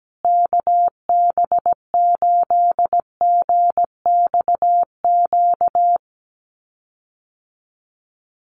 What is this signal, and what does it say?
Morse code, heard as a keyed tone, about 17 wpm, 700 Hz. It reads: KB8GXQ
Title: USA Call Signs Repeated 2x 17wpm